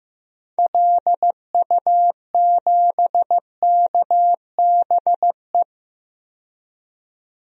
Morse code, heard as a keyed tone, about 15 wpm, 700 Hz. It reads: LU7KBE
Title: DXCC-100 Call Signs - 6 Characters - 15wpm